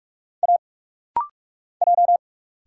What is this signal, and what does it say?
Morse code, heard as a keyed tone, about 44 wpm, 690 Hz. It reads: A J
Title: Single Letter-Number - Rapid-Fire 45wpm